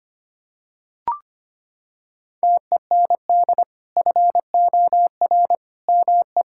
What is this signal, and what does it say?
Morse code, heard as a keyed tone, about 25 wpm, 700 Hz. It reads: TEND FOR ME
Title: Sentences from Top 1000 Words - Review Part 5 - 25wpm